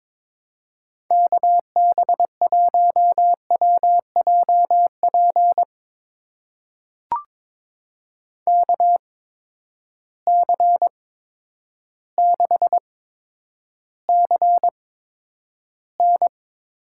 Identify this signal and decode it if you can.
Morse code, heard as a keyed tone, about 22 wpm, spaced as for 10 wpm, 700 Hz. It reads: KB1WJP KC6CN
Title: USA Call Signs - 6 Characters - 8x Character Spacing - 22wpm